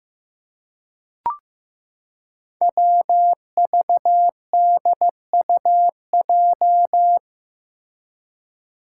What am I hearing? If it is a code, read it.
Morse code, heard as a keyed tone, about 15 wpm, 700 Hz. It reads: WVDUJ
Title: Five Challenging Letter-Number Combinations 15wpm